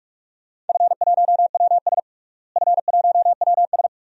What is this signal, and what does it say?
Morse code, heard as a keyed tone, about 47 wpm, 700 Hz. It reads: F1WS F1WS
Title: DXCC-100 Call Signs - 4 Characters - Repeated 2x - 45wpm